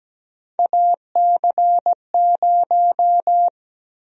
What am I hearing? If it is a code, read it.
Morse code, heard as a keyed tone, about 17 wpm, 700 Hz. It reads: AC0
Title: USA Call Sign Prefixes - 17wpm